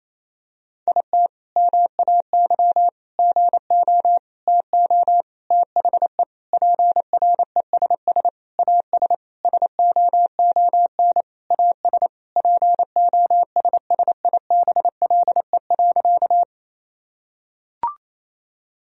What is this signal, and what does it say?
Morse code, heard as a keyed tone, about 28 wpm, 700 Hz. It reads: IT MAY GO TO THE PRESS AS SOON AS POSSIBLE.
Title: Sentences from Top 1000 Words - Review Part 1 - 28wpm